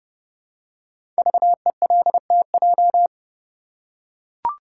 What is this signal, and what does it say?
Morse code, heard as a keyed tone, about 30 wpm, 700 Hz. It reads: VELTJ
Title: Code Groups - 5 Letter:Numbers - 30wpm